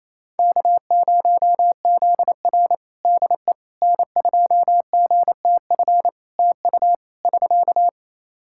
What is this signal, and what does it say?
Morse code, heard as a keyed tone, about 28 wpm, 700 Hz. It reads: K0ZR DE N2GTF TU <SK>
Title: Practice QSO - Exchange 4 - Ending - 28wpm